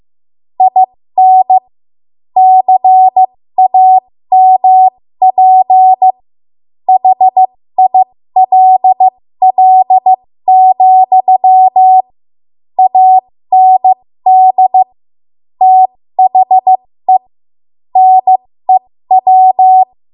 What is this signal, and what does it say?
Morse code, heard as a keyed tone, about 15 wpm, spaced as for 13 wpm, 750 Hz. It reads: IN CAMP HILL, AND THE NEW
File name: Morse / 13WPM